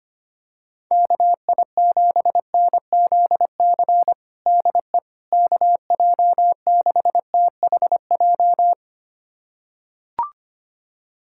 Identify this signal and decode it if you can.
Morse code, heard as a keyed tone, about 25 wpm, 700 Hz. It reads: KI7NZC DE KJ6THJ
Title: USA Callsign DE Callsign - 6 Characters - 25wpm